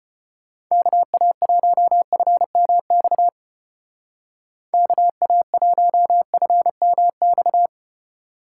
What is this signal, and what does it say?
Morse code, heard as a keyed tone, about 31 wpm, 700 Hz. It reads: KA1FMX KA1FMX